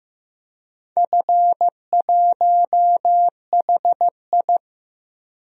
Morse code, heard as a keyed tone, about 15 wpm, 700 Hz. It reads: F1HI
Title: DXCC-100 Call Signs - 4 Characters - Repeated 3x - 15wpm